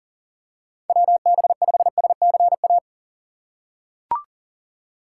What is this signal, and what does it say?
Morse code, heard as a keyed tone, about 40 wpm, 700 Hz. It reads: WB5SCA
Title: USA Call Signs - 6 Characters - 4x Character Spacing - 40wpm